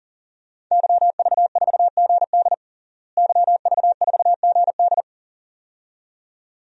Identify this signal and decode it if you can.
Morse code, heard as a keyed tone, about 40 wpm, 700 Hz. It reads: YV4GD YV4GD
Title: DXCC-100 Call Signs - 5 Characters - Repeated 2x - 40wpm